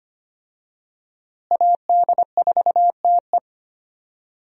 Morse code, heard as a keyed tone, about 25 wpm, 700 Hz. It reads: AD4TE
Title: USA Call Signs - 5 Characters - Repeated 2x - 25wpm